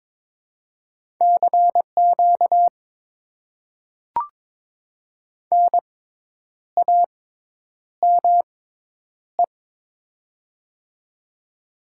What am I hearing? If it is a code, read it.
Morse code, heard as a keyed tone, about 22 wpm, spaced as for 9 wpm, 700 Hz. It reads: CQ NAME
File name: Top 90 QSO Elements - 6x Character Spacing - 22wpm